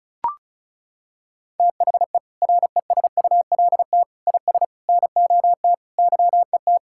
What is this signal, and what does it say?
Morse code, heard as a keyed tone, about 36 wpm, 700 Hz. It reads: THE RESULT IS NOT YET
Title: Sentences from Top 1000 Words - Review Part 5 - 35wpm